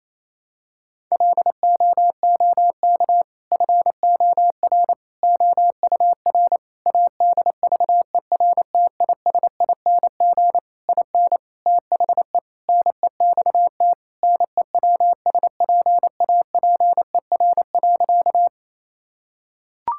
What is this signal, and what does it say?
Morse code, heard as a keyed tone, about 28 wpm, 700 Hz. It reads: LOOK FOR OUR ADVERTISING IN THE NEXT NEWSPAPER.
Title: Sentences from Top 1000 Words - Review Part 6 - 28wpm